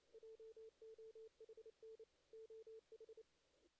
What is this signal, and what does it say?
Morse code, heard as a keyed tone, about 33 wpm, 440 Hz. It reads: JOHN OH
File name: 2016 JAN CW NAQP / 2016 Jan naqp k5ot - 0-6